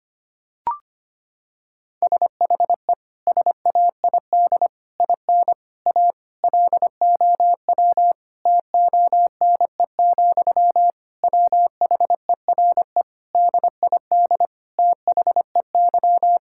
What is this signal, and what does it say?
Morse code, heard as a keyed tone, about 25 wpm, 700 Hz. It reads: SHE SAID IN A LOW TONE, WHERE DID THEY
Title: Sentences from Top 1600 Words 25wpm